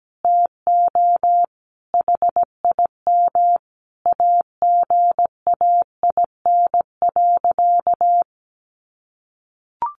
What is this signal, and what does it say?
Morse code, heard as a keyed tone, about 17 wpm, 700 Hz. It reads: TO HIM AGAIN.
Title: Sentences from Top 1500 Words 17wpm